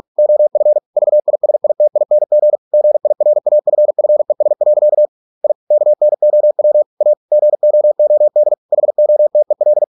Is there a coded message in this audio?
Yes: Morse code from 0.2 to 9.8 s, heard as a keyed tone, about 46 wpm, 600 Hz. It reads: OF VISITING GIRAFFES. I KNOW A GOOD HOTEL